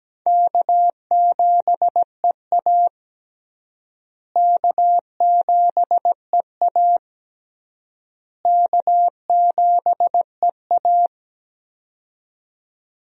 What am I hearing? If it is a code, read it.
Morse code, heard as a keyed tone, about 17 wpm, 700 Hz. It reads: K7EA K7EA K7EA
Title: USA Call Signs - 4 Characters - Repeated 3x - 17wpm